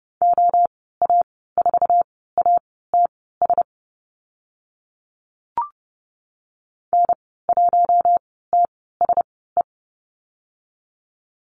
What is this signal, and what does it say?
Morse code, heard as a keyed tone, about 30 wpm, spaced as for 17 wpm, 700 Hz. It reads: OA4ATS N1TSE
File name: Lesson 14g - Number 2 - Call Signs 30wpm